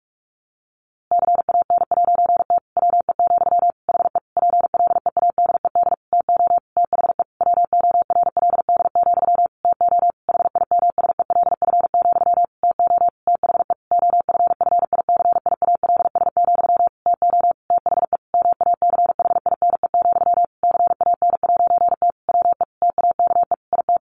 Morse code, heard as a keyed tone, about 45 wpm, 700 Hz. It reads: CAN'T WE, HE PLEADED TO THE WORLD, TO HIMSELF, TO THE OFFICIALS, TO THE MACHINE, CAN'T WE TAKE IT
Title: A Sound Of Thunder 45wpm